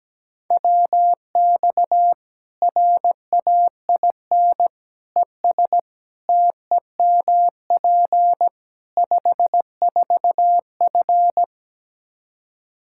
Morse code, heard as a keyed tone, about 17 wpm, 700 Hz. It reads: WX RAIN ES TEMP 54F
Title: Practice QSO - Exchange 2 - Info - 17wpm